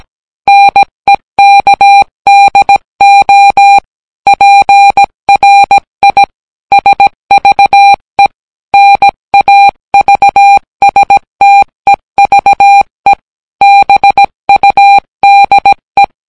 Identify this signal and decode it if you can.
Morse code, heard as a keyed tone, about 17 wpm, 800 Hz. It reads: NEKDO PRI SVE NAVSTEVE BUDE